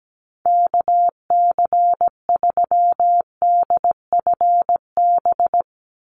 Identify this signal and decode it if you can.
Morse code, heard as a keyed tone, about 17 wpm, 700 Hz. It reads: KC3DFB
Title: USA Call Signs 17wpm